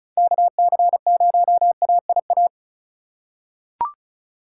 Morse code, heard as a keyed tone, about 36 wpm, 700 Hz. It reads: KC0AIA
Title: USA Call Signs - 6 Characters - 2x Character Spacing - 35wpm